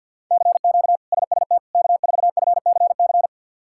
Morse code, heard as a keyed tone, about 50 wpm, 700 Hz. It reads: CQ SST K4FCC
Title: K1USN SST QSO Practice - 50wpm